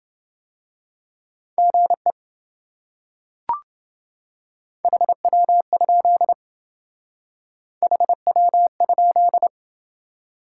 Morse code, heard as a keyed tone, about 28 wpm, 700 Hz. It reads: GE HW? HW?